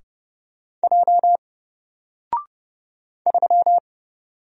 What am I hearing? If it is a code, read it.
Morse code, heard as a keyed tone, about 30 wpm, 700 Hz. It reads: J 3
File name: Single-Letter-Number-30wpm